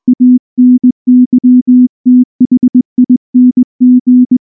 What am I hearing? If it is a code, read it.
Morse code, heard as a keyed tone, about 19 wpm, 250 Hz. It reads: ANYTHING
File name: morse / morse-training-20WPM-4